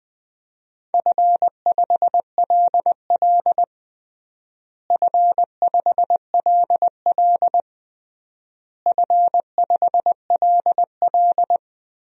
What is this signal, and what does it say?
Morse code, heard as a keyed tone, about 20 wpm, 700 Hz. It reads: F5LL F5LL F5LL
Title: DXCC-100 Call Signs - 4 Characters - Repeated 3x - 20wpm